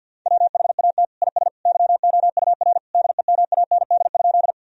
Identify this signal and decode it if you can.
Morse code, heard as a keyed tone, about 52 wpm, 700 Hz. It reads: WHAT IS YOUR DEMAND?
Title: Sentences from Top 1000 Words - Review Part 4 - 50wpm